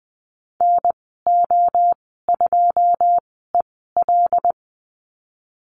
Morse code, heard as a keyed tone, about 20 wpm, spaced as for 15 wpm, 700 Hz. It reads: NO2EL
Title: 088 Lesson 15g - Number 5 - Call Signs 20wpm